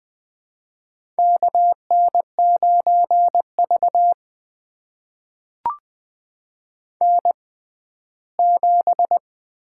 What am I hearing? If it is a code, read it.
Morse code, heard as a keyed tone, about 20 wpm, spaced as for 14 wpm, 700 Hz. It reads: KN9V N7
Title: USA Call Signs - 4 Characters - 6x Character Spacing - 20wpm